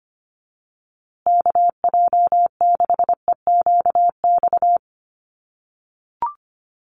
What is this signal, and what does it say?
Morse code, heard as a keyed tone, about 25 wpm, 700 Hz. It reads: KJ6EQX
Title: USA Call Signs Repeated 2x 25wpm